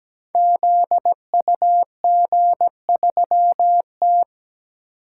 Morse code, heard as a keyed tone, about 17 wpm, 700 Hz. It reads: ZUG3T
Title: Code Groups - 5 Letter:Numbers - 17wpm